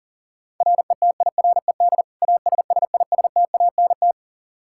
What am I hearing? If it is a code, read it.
Morse code, heard as a keyed tone, about 40 wpm, 700 Hz. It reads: RETIRED ASSISTANT
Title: Practice QSO - Exchange 3 - More Info - 40wpm